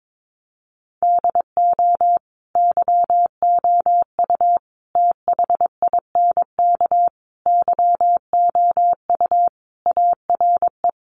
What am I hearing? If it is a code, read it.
Morse code, heard as a keyed tone, about 22 wpm, 700 Hz. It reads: DO YOU THINK YOU ARE
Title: Sentences from Top 1200 Words 22wpm